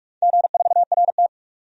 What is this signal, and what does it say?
Morse code, heard as a keyed tone, about 47 wpm, 700 Hz. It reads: G4RT
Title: DXCC-100 Call Signs - 4 Characters - 45wpm